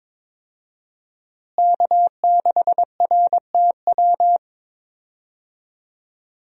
Morse code, heard as a keyed tone, about 22 wpm, 700 Hz. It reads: K6RTW
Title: USA Call Signs - 5 Characters - Repeated 2x - 22wpm